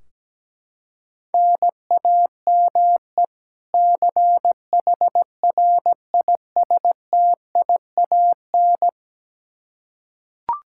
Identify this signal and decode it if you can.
Morse code, heard as a keyed tone, about 17 wpm, 700 Hz. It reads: NAME CHRISTIAN
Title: Practice QSO - Exchange 1 - Beginning - 17wpm